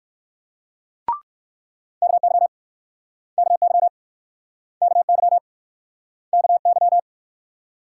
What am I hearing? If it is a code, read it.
Morse code, heard as a keyed tone, about 61 wpm, 700 Hz. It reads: KY KY KY KY